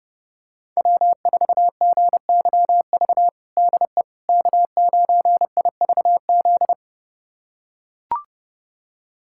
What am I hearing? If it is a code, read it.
Morse code, heard as a keyed tone, about 30 wpm, 700 Hz. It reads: W4GYV DE K9IVZ
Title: USA Callsign DE Callsign - 5 Characters - 30wpm